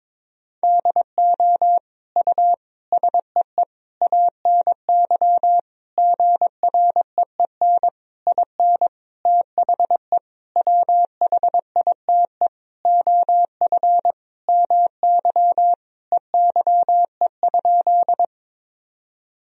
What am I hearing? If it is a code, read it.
Morse code, heard as a keyed tone, about 22 wpm, 700 Hz. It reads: DO U SEE ANY GREEN IN THE WHITE OF MY EYE?
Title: Sentences from Top 500 Words and Abbreviations 22wpm